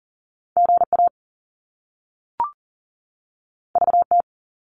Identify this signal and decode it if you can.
Morse code, heard as a keyed tone, about 38 wpm, 700 Hz. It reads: GA VT
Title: US States 40wpm